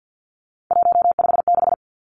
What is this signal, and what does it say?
Morse code, heard as a keyed tone, about 50 wpm, 700 Hz. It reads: 156
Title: Numbers - 3-Digit 50wpm